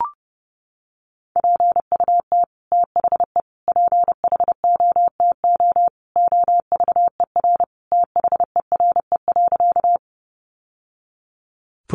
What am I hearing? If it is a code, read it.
Morse code, heard as a keyed tone, about 28 wpm, 700 Hz. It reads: PUT THE PHOTO OVER THERE.
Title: Sentences from Top 1200 Words 30wpm